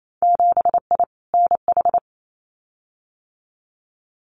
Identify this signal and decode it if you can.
Morse code, heard as a keyed tone, about 28 wpm, 700 Hz. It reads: 7I NH
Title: Winter Field Day Exchanges Encore 28wpm